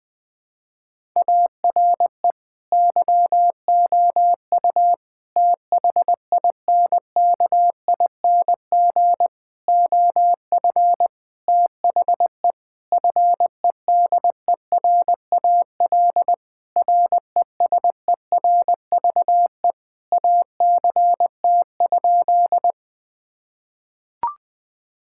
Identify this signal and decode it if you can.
Morse code, heard as a keyed tone, about 21 wpm, 700 Hz. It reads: ARE YOU THINKING OF THE FEDERAL RESERVE ACT?
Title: Sentences from Top 1000 Words - Review Part 6 - 20wpm